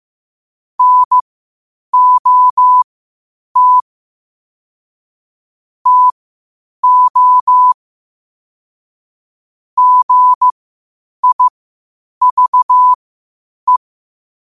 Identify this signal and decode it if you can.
Morse code, heard as a keyed tone, about 15 wpm, spaced as for 8 wpm, 1000 Hz. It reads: NOT TO GIVE
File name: qotdfiles / 8wpm